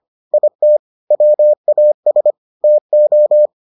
Morse code, heard as a keyed tone, about 25 wpm, 600 Hz. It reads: IT WAS TO